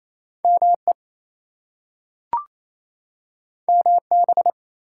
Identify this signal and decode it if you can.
Morse code, heard as a keyed tone, about 28 wpm, 700 Hz. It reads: ME MB
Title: US States and Canadian Provinces 28wpm